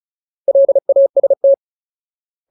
Morse code, heard as a keyed tone, about 34 wpm, 550 Hz. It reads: LAST
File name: Ultimate Word List / 35WPM Ultimate